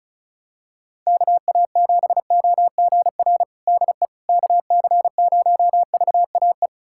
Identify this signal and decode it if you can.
Morse code, heard as a keyed tone, about 36 wpm, 700 Hz. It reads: KA7OGR DE KC0VAE
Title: USA Callsign DE Callsign - 6 Characters - 35wpm